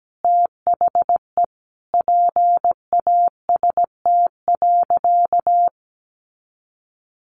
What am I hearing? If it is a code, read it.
Morse code, heard as a keyed tone, about 17 wpm, 700 Hz. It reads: THE PAST.
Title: A Sound Of Thunder 17wpm